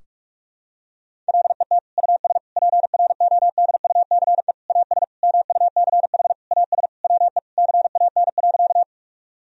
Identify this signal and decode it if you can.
Morse code, heard as a keyed tone, about 43 wpm, 700 Hz. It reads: LET US PRODUCE AS MUCH AS WE CAN.
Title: Sentences from Top 1000 Words - Review Part 3 - 45wpm